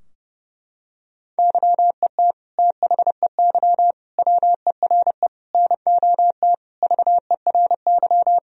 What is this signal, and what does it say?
Morse code, heard as a keyed tone, about 30 wpm, 700 Hz. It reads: YET THEY WERE NOT VERY
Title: Sentences from Top 1000 Words - Review Part 3 - 30wpm